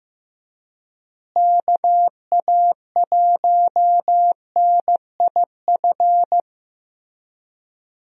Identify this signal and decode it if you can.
Morse code, heard as a keyed tone, about 15 wpm, 700 Hz. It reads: KA1NIF
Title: USA Call Signs - 6 Characters - Repeated 3x - 15wpm